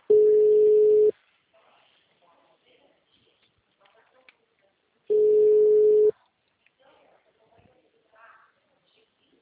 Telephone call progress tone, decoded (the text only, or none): ringback tone